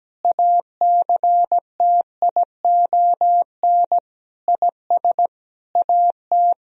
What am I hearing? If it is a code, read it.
Morse code, heard as a keyed tone, about 17 wpm, 700 Hz. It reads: ACTION IS AT
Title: Sentences from Top 1000 Words - Review Part 4 - 17wpm